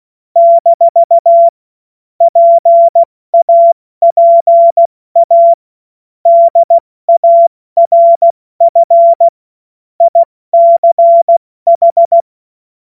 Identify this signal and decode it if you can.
Morse code, heard as a keyed tone, about 16 wpm, 680 Hz. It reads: - PAPA DARF ICH
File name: Morsekurs / Witze / Witz 07. September 2021, 70 F80